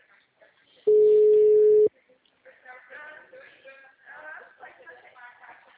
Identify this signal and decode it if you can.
Ringback tone (Europe, 425 Hz).